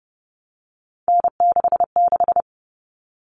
Morse code, heard as a keyed tone, about 30 wpm, 700 Hz. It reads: N66
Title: All Cut Numbers - 3-Digit 30wpm